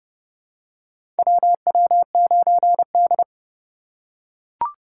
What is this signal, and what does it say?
Morse code, heard as a keyed tone, about 32 wpm, 700 Hz. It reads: WW9D
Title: USA Call Signs - 4 Characters - 4x Character Spacing - 30wpm